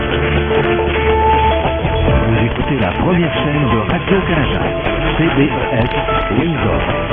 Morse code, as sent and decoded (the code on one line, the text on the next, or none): none